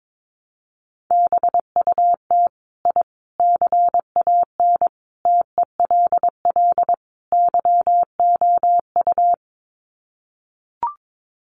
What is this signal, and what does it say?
Morse code, heard as a keyed tone, about 22 wpm, 700 Hz. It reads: BUT I CAN TELL YOU
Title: Sets of 5 Words Encore 22wpm